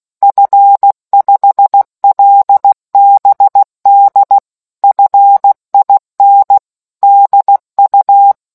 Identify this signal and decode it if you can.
Morse code, heard as a keyed tone, about 16 wpm, 800 Hz. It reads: F5LBD FIN DU